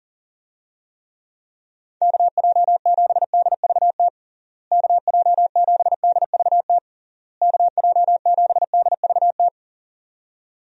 Morse code, heard as a keyed tone, about 40 wpm, 700 Hz. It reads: KJ7DVT KJ7DVT KJ7DVT
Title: USA Call Signs - 6 Characters - Repeated 3x - 40wpm